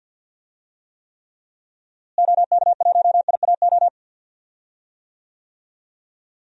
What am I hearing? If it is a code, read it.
Morse code, heard as a keyed tone, about 50 wpm, 700 Hz. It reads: OK1IAO
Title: DXCC-100 Call Signs - 6 Characters - Repeated 3x - 50wpm